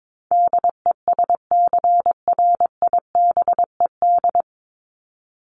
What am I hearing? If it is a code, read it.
Morse code, heard as a keyed tone, about 22 wpm, 700 Hz. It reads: DESCRIBED
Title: ED Suffix - Top 100 Words with Base Words 22wpm